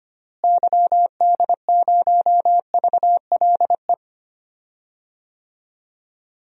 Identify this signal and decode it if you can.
Morse code, heard as a keyed tone, about 25 wpm, 700 Hz. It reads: YD0VLE
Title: DXCC-100 Call Signs - 6 Characters - Repeated 3x - 25wpm